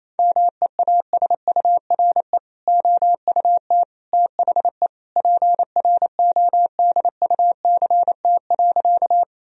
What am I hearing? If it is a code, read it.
Morse code, heard as a keyed tone, about 28 wpm, 700 Hz. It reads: MEASURE OUT THE PRODUCT.
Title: Sentences from Top 1000 Words - Review Part 2 - 28wpm